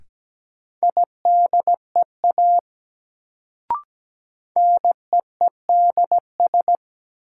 Morse code, heard as a keyed tone, about 17 wpm, 700 Hz. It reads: IDEA NEEDS
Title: Sets of 3 Words - New Word Drill - 17wpm